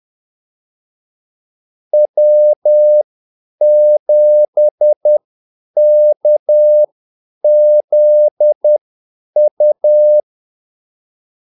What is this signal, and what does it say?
Morse code, heard as a keyed tone, about 10 wpm, spaced as for 8 wpm, 600 Hz. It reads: W7KZU
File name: Callsigns-10-8wpm